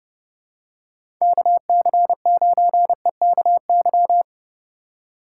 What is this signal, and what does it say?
Morse code, heard as a keyed tone, about 32 wpm, 700 Hz. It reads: KC9EKY
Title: USA Call Signs - 6 Characters - 2x Character Spacing - 30wpm